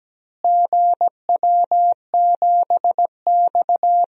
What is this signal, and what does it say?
Morse code, heard as a keyed tone, about 17 wpm, 700 Hz. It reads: GW7X
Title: DXCC-100 Call Signs - 4 Characters - Repeated 3x - 17wpm